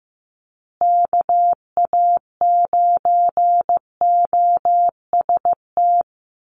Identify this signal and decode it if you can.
Morse code, heard as a keyed tone, about 15 wpm, 700 Hz. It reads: KA9OST
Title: USA Call Signs Repeated 2x 15wpm